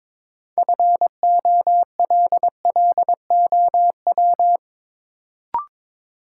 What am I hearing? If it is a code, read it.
Morse code, heard as a keyed tone, about 23 wpm, 700 Hz. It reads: FOLLOW
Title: Sentences from Top 200 Words - New Word Drill - 22wpm